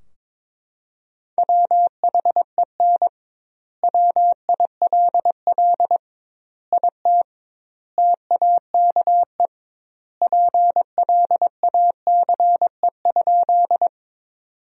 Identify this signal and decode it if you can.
Morse code, heard as a keyed tone, about 22 wpm, 700 Hz. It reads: WHEN WILL IT TAKE PLACE?
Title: 2x Word-Spacing - Sentences from Top 200 Words - 22wpm